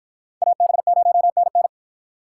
Morse code, heard as a keyed tone, about 53 wpm, 700 Hz. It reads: AB0NN